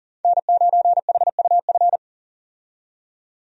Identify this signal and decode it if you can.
Morse code, heard as a keyed tone, about 40 wpm, 700 Hz. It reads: N9HUF
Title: USA Call Signs - 5 Characters - 40wpm